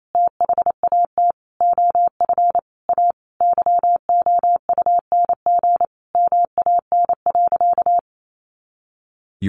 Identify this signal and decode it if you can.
Morse code, heard as a keyed tone, about 28 wpm, 700 Hz. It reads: THAT OF A YOUNG MAN.
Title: Sentences from Top 500 Words 28wpm